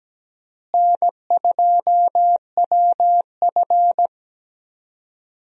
Morse code, heard as a keyed tone, about 17 wpm, 700 Hz. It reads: N2WF
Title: USA Call Signs - 4 Characters - Repeated 2x - 17wpm